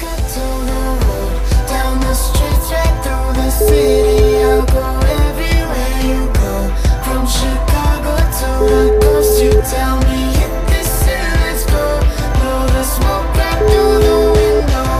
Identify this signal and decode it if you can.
Ringback tone (Europe, 425 Hz), with noise.